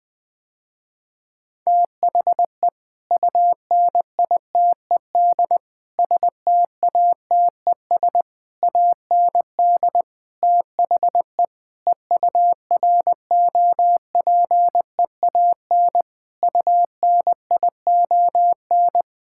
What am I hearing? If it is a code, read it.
Morse code, heard as a keyed tone, about 20 wpm, 700 Hz. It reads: THE UNITED STATES AND THE EUROPEAN UNION
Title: Sentences with 5-Word Phrases - Spoken Beginning - 20wpm